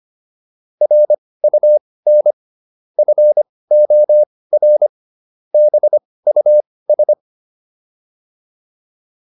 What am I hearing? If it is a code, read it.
Morse code, heard as a keyed tone, about 25 wpm, spaced as for 18 wpm, 600 Hz. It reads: RUN FOR BUS